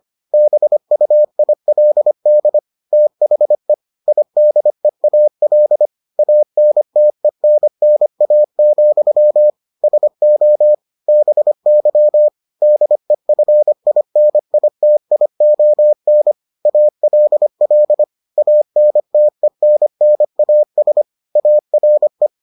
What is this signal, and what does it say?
Morse code, heard as a keyed tone, about 25 wpm, 600 Hz. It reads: BUILD THE IDEAL ANTENNA, SO BY DEFINITION ALL ANTENNAS ARE